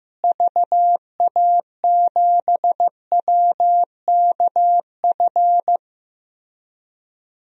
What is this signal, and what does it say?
Morse code, heard as a keyed tone, about 15 wpm, 700 Hz. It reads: VA7WKF
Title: DXCC-100 Call Signs - 6 Characters - Repeated 3x - 15wpm